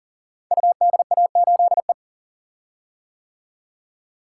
Morse code, heard as a keyed tone, about 38 wpm, 700 Hz. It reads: UDA8E
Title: Code Groups - 5 Letter:Numbers - 40wpm